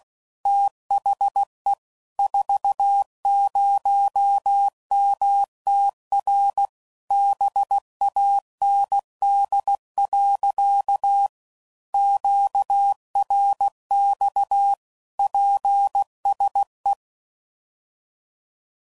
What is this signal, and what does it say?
Morse code, heard as a keyed tone, about 16 wpm, 800 Hz. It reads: THE 40MTR BAND. QRX PSE
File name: CW Code Groups @15WPM / QSOs / kc7sac